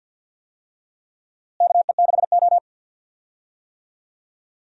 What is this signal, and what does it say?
Morse code, heard as a keyed tone, about 52 wpm, 700 Hz. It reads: KE6O
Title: USA Call Signs - 4 Characters - Repeated 3x - 50wpm